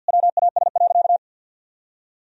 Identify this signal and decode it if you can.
Morse code, heard as a keyed tone, about 50 wpm, 700 Hz. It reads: WAS.
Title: Sentences from Top 1000 Words - Review Part 1 - 50wpm